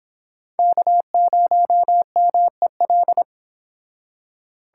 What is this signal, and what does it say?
Morse code, heard as a keyed tone, about 25 wpm, 700 Hz. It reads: K0MEL